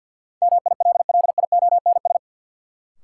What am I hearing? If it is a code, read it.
Morse code, heard as a keyed tone, about 48 wpm, 700 Hz. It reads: MILLIONS
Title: Sets of 5 Words - New Word Drill - 50wpm